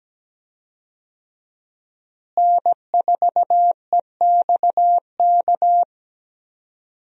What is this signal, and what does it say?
Morse code, heard as a keyed tone, about 17 wpm, 700 Hz. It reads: N4EXK
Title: USA Call Signs - 5 Characters - Repeated 2x - 17wpm